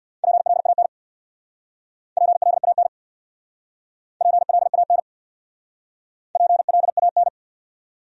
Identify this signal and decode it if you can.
Morse code, heard as a keyed tone, about 62 wpm, 700 Hz. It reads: PLAN PLAN PLAN PLAN
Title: Speed-Racing Top 500 Words 50wpm